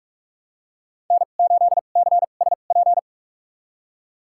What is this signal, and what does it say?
Morse code, heard as a keyed tone, about 46 wpm, spaced as for 32 wpm, 700 Hz. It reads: N8CSP
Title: USA Call Signs - 5 Characters - 2x Character Spacing - 45wpm